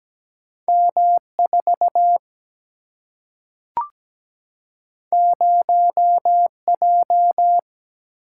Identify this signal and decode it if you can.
Morse code, heard as a keyed tone, about 17 wpm, 700 Hz. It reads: M4 0J
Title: Single-Letter Single-Number 17wpm